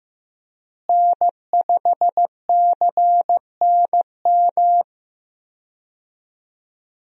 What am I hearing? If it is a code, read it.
Morse code, heard as a keyed tone, about 15 wpm, 700 Hz. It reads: N5CNM
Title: USA Call Signs - 5 Characters - Repeated 2x - 15wpm